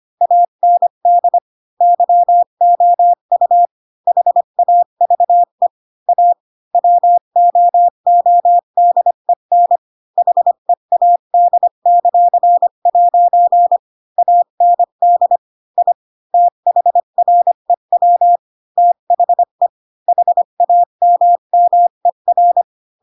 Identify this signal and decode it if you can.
Morse code, heard as a keyed tone, about 25 wpm, 700 Hz. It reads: AND YOU HAVE A WOODEN HEAD;' AND I THREW THE HAMMER
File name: Pinocchio / Pinocchio ch7 20 25 ik1hge